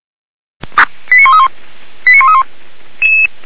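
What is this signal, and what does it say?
Five-tone selective call (ZVEI), heard as quick runs of five tones, tones of about 80 ms. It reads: 89131 89131
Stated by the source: ZVEI five-tone selective call 89110, received on an SDR.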